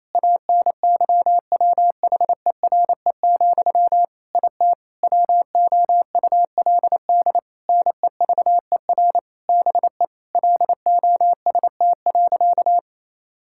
Morse code, heard as a keyed tone, about 28 wpm, 700 Hz. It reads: ANYWHERE, IT WOULD NEVER BE LOST.
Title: Sentences from Top 1600 Words 28wpm